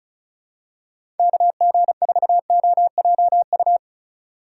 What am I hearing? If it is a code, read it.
Morse code, heard as a keyed tone, about 34 wpm, 700 Hz. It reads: KG4OJU
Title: USA Call Signs - 6 Characters - 2x Character Spacing - 35wpm